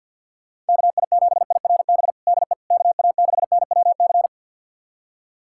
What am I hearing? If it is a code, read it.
Morse code, heard as a keyed tone, about 46 wpm, 700 Hz. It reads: KI7IRB DE KA6NWC
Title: USA Callsign DE Callsign - 6 Characters - 50wpm